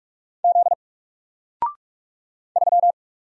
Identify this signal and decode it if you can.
Morse code, heard as a keyed tone, about 44 wpm, 700 Hz. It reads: Z 3
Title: Single Letter-Number Alt 45wpm